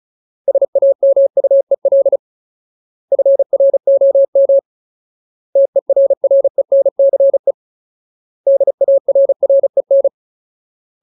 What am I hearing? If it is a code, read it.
Morse code, heard as a keyed tone, about 34 wpm, 550 Hz. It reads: SAMUEL FROM TERRENCE DARREN
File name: Ultimate Word List / 35WPM Ultimate